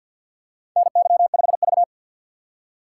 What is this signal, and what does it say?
Morse code, heard as a keyed tone, about 53 wpm, 700 Hz. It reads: NY5V
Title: USA Call Signs - 4 Characters - 2x Character Spacing - 50wpm